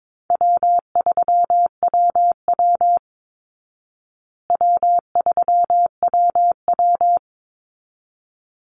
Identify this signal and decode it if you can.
Morse code, heard as a keyed tone, about 22 wpm, 690 Hz. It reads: W3WW W3WW
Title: Easy Call Signs Repeated 2x - 22wpm